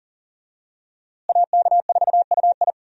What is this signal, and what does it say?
Morse code, heard as a keyed tone, about 39 wpm, 700 Hz. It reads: AK4UI
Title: USA Call Signs - 5 Characters - Repeated 3x - 40wpm